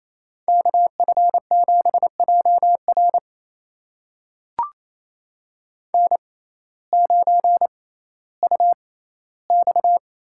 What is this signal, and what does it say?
Morse code, heard as a keyed tone, about 28 wpm, spaced as for 15 wpm, 700 Hz. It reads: KF7JR N9UX
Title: USA Call Signs - 5 Characters - 6x Character Spacing - 28wpm